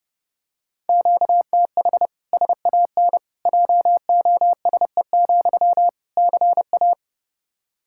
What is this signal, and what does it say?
Morse code, heard as a keyed tone, about 32 wpm, 700 Hz. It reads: QTH SAN JOSE, CA
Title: Practice QSO - Exchange 1 - Beginning - 30wpm